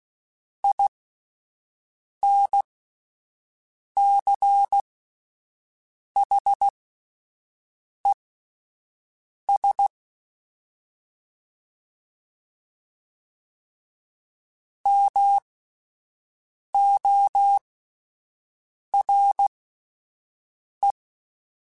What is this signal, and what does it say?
Morse code, heard as a keyed tone, about 16 wpm, spaced as for 5 wpm, 800 Hz. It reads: INCHES MORE
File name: (spaced) / QSOs Spaced-mp3s / WA7QOX 1 @10 sp